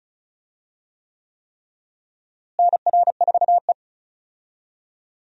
Morse code, heard as a keyed tone, about 34 wpm, 700 Hz. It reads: NR4E
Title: USA Call Signs - 4 Characters - Repeated 3x - 35wpm